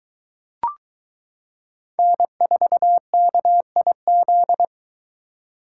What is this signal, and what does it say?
Morse code, heard as a keyed tone, about 22 wpm, 700 Hz. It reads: N4KIZ